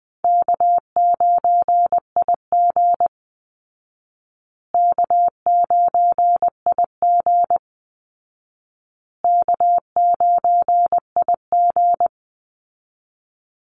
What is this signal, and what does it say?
Morse code, heard as a keyed tone, about 20 wpm, 700 Hz. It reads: K9IG K9IG K9IG
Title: Easy Call Signs Repeated 3x - 20wpm